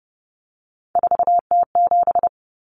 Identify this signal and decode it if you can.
Morse code, heard as a keyed tone, about 30 wpm, 700 Hz. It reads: 4T7
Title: Common Cut Numbers - 3-Digit 30wpm